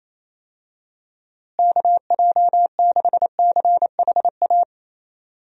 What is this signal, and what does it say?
Morse code, heard as a keyed tone, about 28 wpm, 700 Hz. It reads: KJ6CHA
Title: USA Call Signs - 6 Characters - Repeated 2x - 28wpm